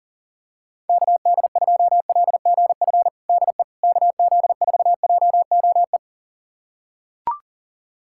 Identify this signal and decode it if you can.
Morse code, heard as a keyed tone, about 40 wpm, 700 Hz. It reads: KD2LGF DE KZ4JOE
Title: USA Callsign DE Callsign - 6 Characters - 40wpm